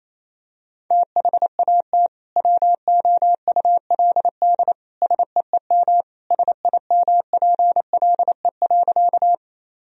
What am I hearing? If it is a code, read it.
Morse code, heard as a keyed tone, about 28 wpm, 700 Hz. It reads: THAT WOULD SEEM SIMPLE.
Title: Sentences from Top 1000 Words - Review Part 1 - 28wpm